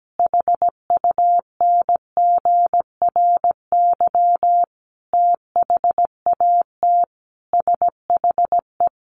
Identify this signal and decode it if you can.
Morse code, heard as a keyed tone, about 17 wpm, 700 Hz. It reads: HUNGRY THAT SHE
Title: Sentences from Top 1400 Words 17wpm